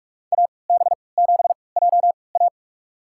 Morse code, heard as a keyed tone, about 46 wpm, spaced as for 25 wpm, 700 Hz. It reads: AB7JA
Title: USA Call Signs - 3x Character Spacing - 45wpm